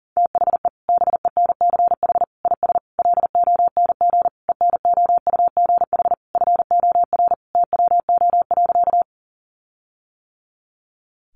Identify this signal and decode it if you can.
Morse code, heard as a keyed tone, about 40 wpm, 700 Hz. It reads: THE BENCH IS LONG ENOUGH FOR TWO.
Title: Sentences from Top 1400 Words 40wpm